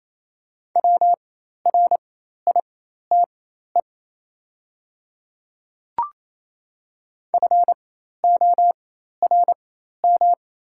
Morse code, heard as a keyed tone, about 27 wpm, spaced as for 13 wpm, 700 Hz. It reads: WRITE FORM
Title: 4x Character Spacing - Top 200 Words - 28wpm